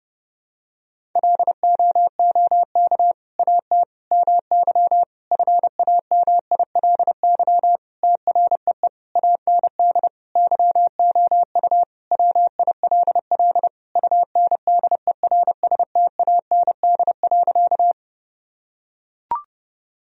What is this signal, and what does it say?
Morse code, heard as a keyed tone, about 30 wpm, 700 Hz. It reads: LOOK AT MY FAMILY TREE AND YOU WILL UNDERSTAND.
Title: Sentences from Top 1000 Words - Review Part 1 - 30wpm